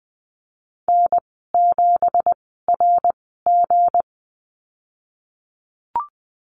Morse code, heard as a keyed tone, about 20 wpm, spaced as for 14 wpm, 700 Hz. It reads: N7RG
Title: 167 Lesson 27g - Number 7 - Call Signs 20wpm